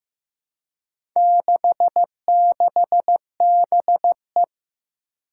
Morse code, heard as a keyed tone, about 15 wpm, 700 Hz. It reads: 66BE
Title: Four Challenging Letter-Number Combinations 15wpm